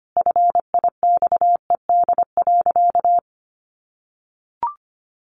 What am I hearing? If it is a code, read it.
Morse code, heard as a keyed tone, about 25 wpm, 700 Hz. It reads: FIXED.
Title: Sentences from Top 1100 Words 25wpm